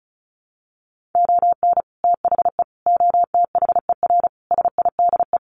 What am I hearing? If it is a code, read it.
Morse code, heard as a keyed tone, about 34 wpm, 700 Hz. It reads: ON THE OTHER SIDE
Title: Sentences from Top 1100 Words 35wpm